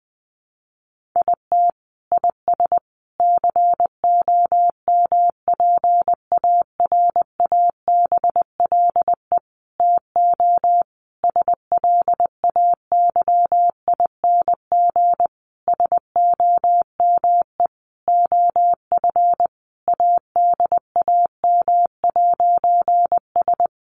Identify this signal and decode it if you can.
Morse code, heard as a keyed tone, about 20 wpm, 700 Hz. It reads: IT IS COMPARABLE TO SLAYING SOME OF ADAM'S
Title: A Sound Of Thunder 20wpm